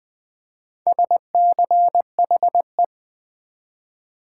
Morse code, heard as a keyed tone, about 20 wpm, 700 Hz. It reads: SCHE
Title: Build A Word - 4 Letters At A Time - 8-Letter Words - 20wpm